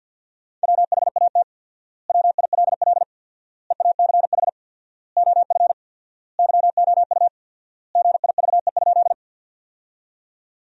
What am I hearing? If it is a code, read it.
Morse code, heard as a keyed tone, about 50 wpm, 700 Hz. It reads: WHAT WILL EACH OF YOU GIVE?
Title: 4x Word-Spacing - Sentences from Top 200 Words - 50wpm